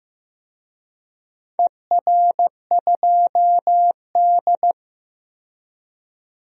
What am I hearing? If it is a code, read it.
Morse code, heard as a keyed tone, about 15 wpm, 700 Hz. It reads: ER2D
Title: DXCC-100 Call Signs - 4 Characters - 15wpm